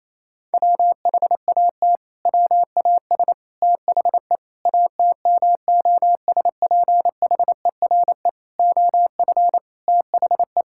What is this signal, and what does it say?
Morse code, heard as a keyed tone, about 28 wpm, 700 Hz. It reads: WHAT WAS THE ATMOSPHERE OF THE
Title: Sentences from Top 1600 Words 28wpm